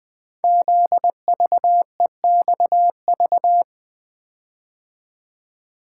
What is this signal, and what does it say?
Morse code, heard as a keyed tone, about 20 wpm, 700 Hz. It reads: ZVEXV
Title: Code Groups - 5 Letters - 20wpm